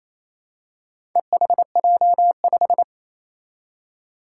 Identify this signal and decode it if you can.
Morse code, heard as a keyed tone, about 27 wpm, 700 Hz. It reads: EHJ5
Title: Four Challenging Letter-Number Combinations 28wpm